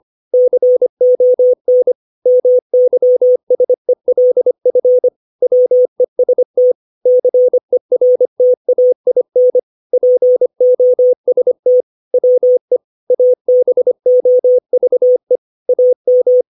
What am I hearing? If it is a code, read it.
Morse code, heard as a keyed tone, about 25 wpm, 500 Hz. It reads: CON MYSELF WEST CERTAIN POST WE ABOVE AM
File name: 632 words / 632 Words 25WPM